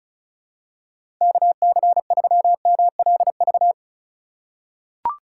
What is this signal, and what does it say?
Morse code, heard as a keyed tone, about 35 wpm, 700 Hz. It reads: KC3MLV
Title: USA Call Signs - 6 Characters - 3x Character Spacing - 35wpm